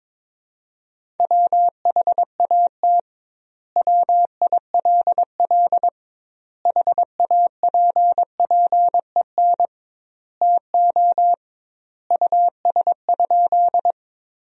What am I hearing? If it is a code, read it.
Morse code, heard as a keyed tone, about 22 wpm, 700 Hz. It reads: WHAT WILL HAPPEN TO US?
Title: 2x Word-Spacing - Sentences from Top 400 Words - 22wpm